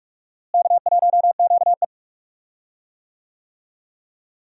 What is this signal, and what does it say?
Morse code, heard as a keyed tone, about 46 wpm, 700 Hz. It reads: K1QE
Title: USA Call Signs - 4 Characters - Repeated 2x - 45wpm